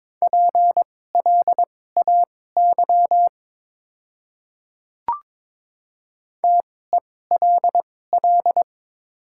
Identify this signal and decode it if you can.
Morse code, heard as a keyed tone, about 22 wpm, spaced as for 16 wpm, 690 Hz. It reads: PLAY TELL
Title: 2x Character Spacing - Top 500 Words - 22wpm